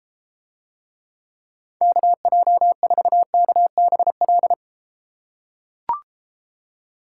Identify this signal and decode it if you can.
Morse code, heard as a keyed tone, about 32 wpm, 700 Hz. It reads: KJ4KBL